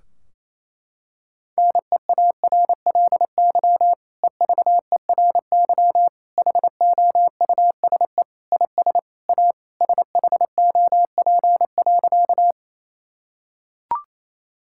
Morse code, heard as a keyed tone, about 28 wpm, 700 Hz. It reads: NEARLY EVERY HOUSE IS A SHOP.
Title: Sentences from Top 1600 Words 28wpm